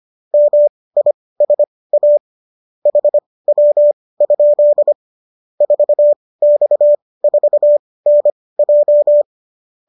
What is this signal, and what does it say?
Morse code, heard as a keyed tone, about 25 wpm, spaced as for 18 wpm, 590 Hz. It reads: MISA HW? 4X4NJ